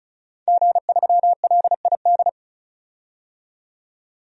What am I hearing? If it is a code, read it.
Morse code, heard as a keyed tone, about 34 wpm, 700 Hz. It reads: G3LID
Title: DXCC-100 Call Signs - 5 Characters - 35wpm